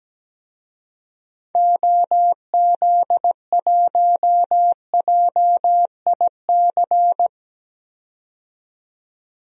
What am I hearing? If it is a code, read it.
Morse code, heard as a keyed tone, about 17 wpm, 700 Hz. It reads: OZ1JIC
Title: DXCC-100 Call Signs - 6 Characters - 17wpm